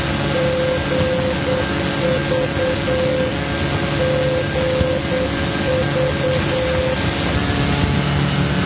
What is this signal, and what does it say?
Morse code, heard as a keyed tone, about 8 wpm, 520 Hz, with noise. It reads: GV GV
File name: GV NDB 1988